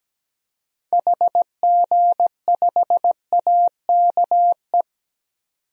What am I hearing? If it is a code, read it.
Morse code, heard as a keyed tone, about 17 wpm, 700 Hz. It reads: HG5AKE
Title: DXCC-100 Call Signs - 6 Characters - 17wpm